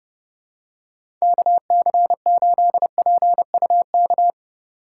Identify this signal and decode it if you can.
Morse code, heard as a keyed tone, about 31 wpm, 700 Hz. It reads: KC8PUK
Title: USA Call Signs - 6 Characters - 2x Character Spacing - 30wpm